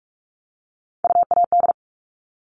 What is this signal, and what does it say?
Morse code, heard as a keyed tone, about 44 wpm, 700 Hz. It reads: UAD
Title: All Cut Numbers - 3-Digit 45wpm